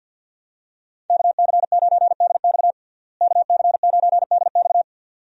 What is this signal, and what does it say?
Morse code, heard as a keyed tone, about 51 wpm, 700 Hz. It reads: KC9DX KC9DX
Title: USA Call Signs - 5 Characters - Repeated 2x - 50wpm